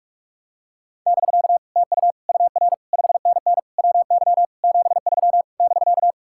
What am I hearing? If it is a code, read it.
Morse code, heard as a keyed tone, about 44 wpm, 700 Hz. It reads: <BK> TU UR 5NN WY 73 <BK>